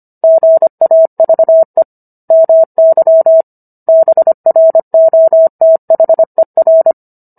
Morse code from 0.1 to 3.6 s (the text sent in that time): GAVE MY